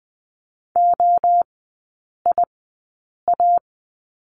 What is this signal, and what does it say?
Morse code, heard as a keyed tone, about 20 wpm, spaced as for 8 wpm, 700 Hz. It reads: OIA
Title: 067 Lesson 12e - Letter d - Three Characters 20wpm